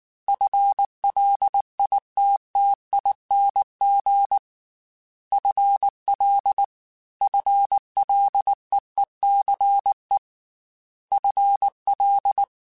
Morse code, heard as a keyed tone, about 19 wpm, 800 Hz. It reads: FLITTING FL FLEECE FL
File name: FL / FL mp3s / FL Prefixes 1x1 @18